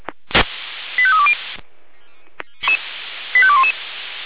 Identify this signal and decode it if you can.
Five-tone selective call (ZVEI), heard as quick runs of five tones, tones of about 70 ms. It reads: 85320 85211